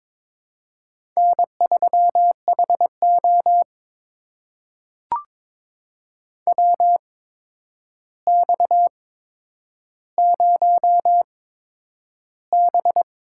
Morse code, heard as a keyed tone, about 22 wpm, spaced as for 9 wpm, 700 Hz. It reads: N3HO WX0B
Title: USA Call Signs - 4 Characters - 8x Character Spacing - 22wpm